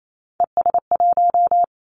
Morse code, heard as a keyed tone, about 28 wpm, 700 Hz. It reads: ES1
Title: Three Challenging Letter-Number Combinations 28wpm